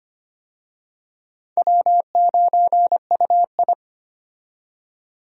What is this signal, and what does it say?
Morse code, heard as a keyed tone, about 25 wpm, 700 Hz. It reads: W9UI
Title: USA Call Signs - 4 Characters - 25wpm